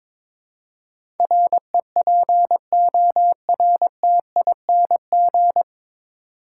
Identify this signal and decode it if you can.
Morse code, heard as a keyed tone, about 22 wpm, 700 Hz. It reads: REPORTING
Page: Morse Code Ninja practice 22wpm